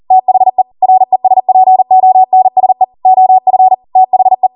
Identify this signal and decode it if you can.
Morse code, heard as a keyed tone, about 40 wpm, 750 Hz. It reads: THE RESPONSE OF THE